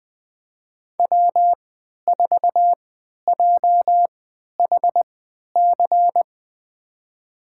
Morse code, heard as a keyed tone, about 21 wpm, spaced as for 11 wpm, 700 Hz. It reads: W4JHC
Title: USA Call Signs - 5 Characters - 3x Character Spacing - 20wpm